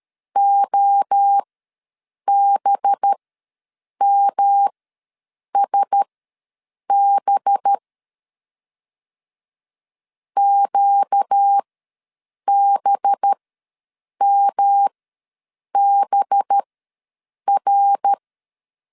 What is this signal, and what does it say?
Morse code, heard as a keyed tone, about 13 wpm, spaced as for 7 wpm, 800 Hz. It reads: OBMSB QBMBR
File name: LEZIONI / 5 LBVC / 5 LB ws 12wpm 3034